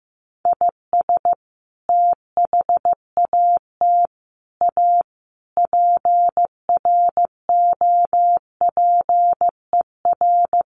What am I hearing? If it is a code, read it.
Morse code, heard as a keyed tone, about 14 wpm, 700 Hz. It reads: IS THAT A PROPER
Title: Sentences from Top 1100 Words 15wpm